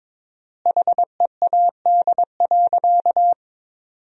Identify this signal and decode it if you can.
Morse code, heard as a keyed tone, about 22 wpm, 700 Hz. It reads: HEAD.
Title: 4x Word-Spacing - Sentences from Top 300 Words - 22wpm